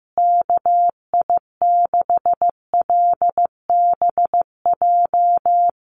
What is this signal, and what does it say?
Morse code, heard as a keyed tone, about 15 wpm, 700 Hz. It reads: KI6LBJ
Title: USA Call Signs Repeated 2x 15wpm